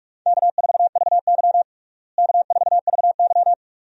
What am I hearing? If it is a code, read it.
Morse code, heard as a keyed tone, about 46 wpm, 700 Hz. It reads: K4VY K4VY
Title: USA Call Signs - 4 Characters - Repeated 2x - 45wpm